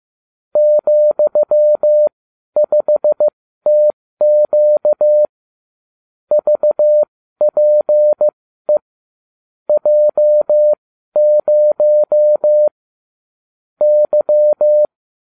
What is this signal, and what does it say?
Morse code, heard as a keyed tone, about 15 wpm, spaced as for 12 wpm, 600 Hz. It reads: ,5TQ VPE J0 Y